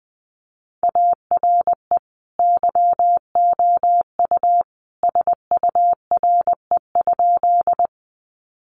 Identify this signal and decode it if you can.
Morse code, heard as a keyed tone, about 20 wpm, 700 Hz. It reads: ARE YOU SURE?
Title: Sentences from Top 300 Words 20wpm